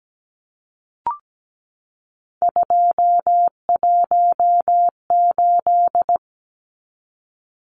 Morse code, heard as a keyed tone, about 17 wpm, 700 Hz. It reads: 218
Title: Numbers - 3-Digit 17wpm